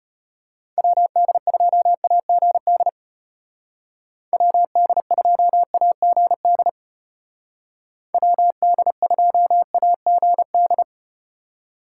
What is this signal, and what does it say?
Morse code, heard as a keyed tone, about 33 wpm, 700 Hz. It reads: WD2AGD WD2AGD WD2AGD